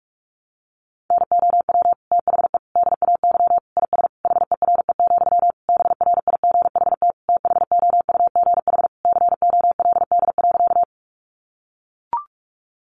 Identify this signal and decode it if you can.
Morse code, heard as a keyed tone, about 44 wpm, 700 Hz. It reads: NOW THE DAY IS HERE, BRIGHT THOUGH COLD.
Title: Sentences from Top 1000 Words 45wpm